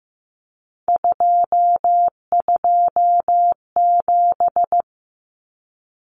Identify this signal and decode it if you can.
Morse code, heard as a keyed tone, about 15 wpm, 700 Hz. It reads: 227
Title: Common Cut Numbers - 3-Digit 15wpm